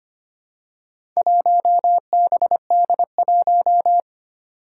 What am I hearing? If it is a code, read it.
Morse code, heard as a keyed tone, about 25 wpm, 700 Hz. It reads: 1BD1
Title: Four Challenging Letter-Number Combinations 25wpm